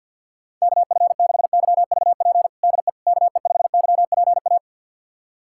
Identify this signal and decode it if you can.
Morse code, heard as a keyed tone, about 46 wpm, 700 Hz. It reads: KF6YVP DE KE5YPA
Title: USA Callsign DE Callsign 50wpm